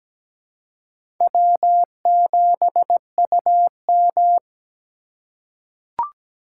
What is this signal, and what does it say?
Morse code, heard as a keyed tone, about 17 wpm, 700 Hz. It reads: W7UM
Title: USA Call Signs - 4 Characters - 2x Character Spacing - 17wpm